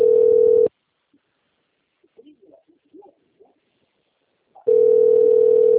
Ringback tone (North America), 440 + 480 Hz.
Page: 191.252.59.82 Recife - PE